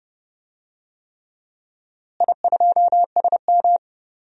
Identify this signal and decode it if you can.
Morse code, heard as a keyed tone, about 30 wpm, 700 Hz. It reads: I2SM
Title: DXCC-100 Call Signs - 4 Characters - Repeated 3x - 30wpm